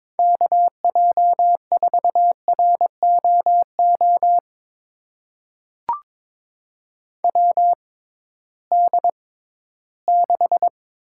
Morse code, heard as a keyed tone, about 22 wpm, spaced as for 14 wpm, 700 Hz. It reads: KJ4ROO WD6
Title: USA Call Signs - 6 Characters - 6x Character Spacing - 22wpm